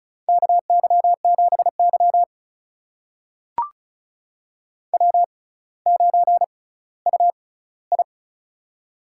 Morse code, heard as a keyed tone, about 36 wpm, spaced as for 18 wpm, 700 Hz. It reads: KY7Y W9UI
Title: USA Call Signs - 4 Characters - 6x Character Spacing - 35wpm